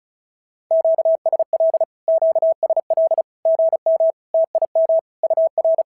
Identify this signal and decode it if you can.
Morse code, heard as a keyed tone, about 36 wpm, 650 Hz. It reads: QSL QSL GM TIM UR
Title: SOTA QSO Practice 35wpm